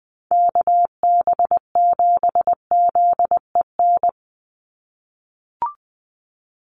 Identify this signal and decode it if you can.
Morse code, heard as a keyed tone, about 20 wpm, 700 Hz. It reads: KB7ZEN
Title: USA Call Signs 20wpm